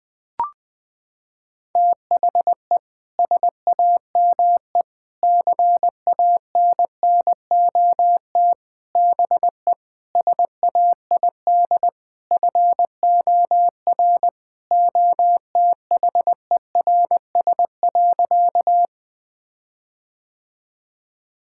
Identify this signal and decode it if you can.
Morse code, heard as a keyed tone, about 21 wpm, 700 Hz. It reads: THE SAME CANNOT BE SAID FOR OTHERS.
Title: Sentences with 6-Word Phrases - 20wpm